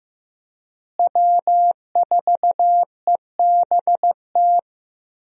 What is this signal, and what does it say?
Morse code, heard as a keyed tone, about 15 wpm, 700 Hz. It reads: W4EBT
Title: USA Call Signs - 5 Characters - Repeated 3x - 15wpm